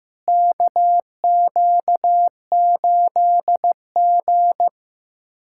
Morse code, heard as a keyed tone, about 15 wpm, 700 Hz. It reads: KQ8G
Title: USA Callsign DE Callsign - 4 Characters - 15wpm